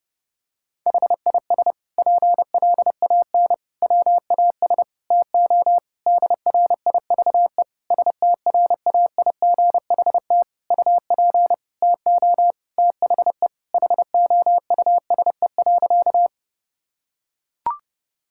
Morse code, heard as a keyed tone, about 30 wpm, 700 Hz. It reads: HIS PLAN WAS TO DRIVE STRAIGHT UP TO THE HOUSE.
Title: Sentences from Top 1000 Words - Review Part 6 - 30wpm